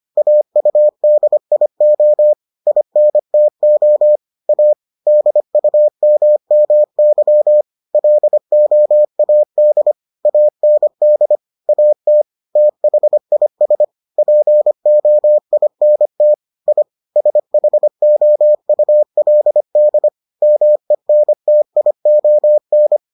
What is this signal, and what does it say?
Morse code, heard as a keyed tone, about 25 wpm, 600 Hz. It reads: AUDIO INTO A DUMMY LOAD AND AT THIS POINT I SHOULD MENTION